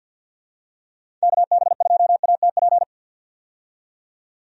Morse code, heard as a keyed tone, about 48 wpm, 700 Hz. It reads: KB2ATP
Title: USA Call Signs - 6 Characters - 50wpm